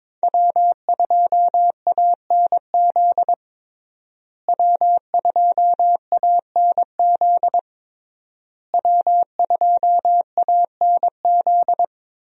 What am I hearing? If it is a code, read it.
Morse code, heard as a keyed tone, about 22 wpm, 700 Hz. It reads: W2ANZ W2ANZ W2ANZ
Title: USA Call Signs - 5 Characters - Repeated 3x - 22wpm